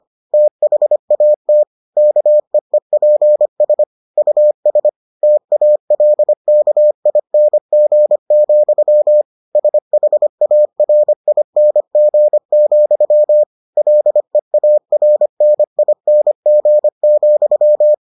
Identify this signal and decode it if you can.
Morse code, heard as a keyed tone, about 25 wpm, 600 Hz. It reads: THAT KEEPS US TALKING, SHARING, LEARNING,